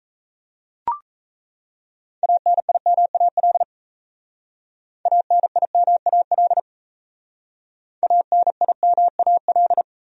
Morse code, heard as a keyed tone, about 36 wpm, 700 Hz. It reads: ANIMAL ANIMAL ANIMAL